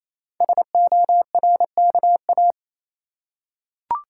Morse code, heard as a keyed tone, about 29 wpm, 700 Hz. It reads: SORKA
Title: Code Groups - 5 Letters - 28wpm